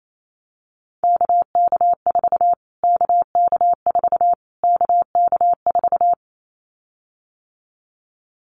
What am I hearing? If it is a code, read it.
Morse code, heard as a keyed tone, about 28 wpm, 700 Hz. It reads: KK4 KK4 KK4
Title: Callsign Prefixes for Top 25 DXCC - Repeated 3x - 28wpm